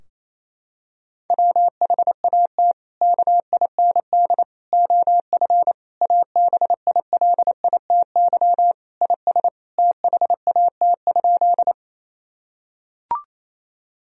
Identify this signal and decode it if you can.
Morse code, heard as a keyed tone, about 28 wpm, 700 Hz. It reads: WHAT KIND OF ABILITY IS THAT?
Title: Sentences from Top 1000 Words - Review Part 3 - 28wpm